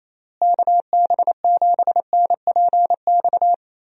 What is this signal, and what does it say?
Morse code, heard as a keyed tone, about 28 wpm, 700 Hz. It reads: KB7NPX
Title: USA Call Signs - 6 Characters - Repeated 3x - 28wpm